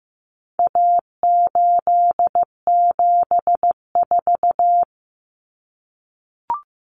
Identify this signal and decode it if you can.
Morse code, heard as a keyed tone, about 15 wpm, 700 Hz. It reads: A874
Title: Common Cut Numbers - 4-Digit 15wpm